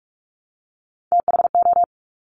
Morse code, heard as a keyed tone, about 46 wpm, 700 Hz. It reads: THO
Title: Top Three Letter Combinations in Words 45wpm